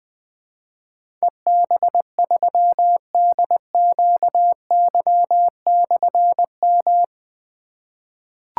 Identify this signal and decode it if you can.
Morse code, heard as a keyed tone, about 20 wpm, 700 Hz. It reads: EB3DQY/M
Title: Difficult Call Signs 20wpm